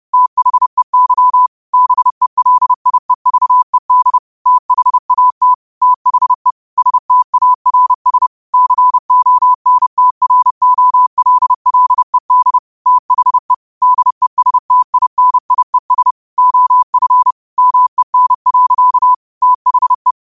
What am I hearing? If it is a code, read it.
Morse code, heard as a keyed tone, about 30 wpm, 1000 Hz. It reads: THEY BELIEVED THAT THE STARS CONTROLLED THE DESTINIES OF MEN. THE
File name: qotdfiles / 30wpm